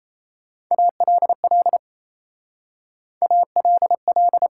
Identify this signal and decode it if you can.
Morse code, heard as a keyed tone, about 36 wpm, 700 Hz. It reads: ALL ALL